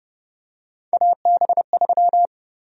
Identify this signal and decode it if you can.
Morse code, heard as a keyed tone, about 30 wpm, 700 Hz. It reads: AB3
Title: USA Call Sign Prefixes - 30wpm